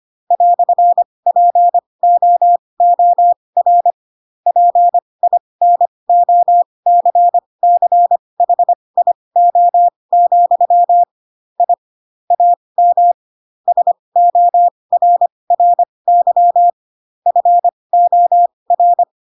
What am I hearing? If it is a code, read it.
Morse code, heard as a keyed tone, about 25 wpm, spaced as for 20 wpm, 700 Hz. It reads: "POOR PINOCCHIO, I AM SORRY FOR
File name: Pinocchio / Pinocchio ch4 15 25 ik1hge